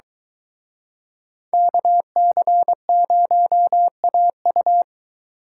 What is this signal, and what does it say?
Morse code, heard as a keyed tone, about 23 wpm, 700 Hz. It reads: KC0AU